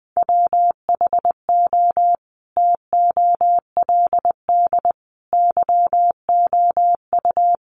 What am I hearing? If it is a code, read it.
Morse code, heard as a keyed tone, about 20 wpm, 700 Hz. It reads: WHO TOLD YOU
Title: Sentences from Top 400 Words 20wpm